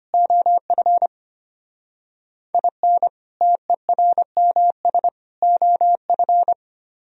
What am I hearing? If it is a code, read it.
Morse code, heard as a keyed tone, about 32 wpm, 700 Hz. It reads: OF IN TERMS OF